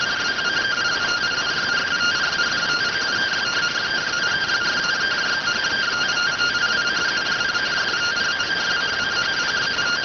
Two-tone teleprinter signal, about 170 Hz shift.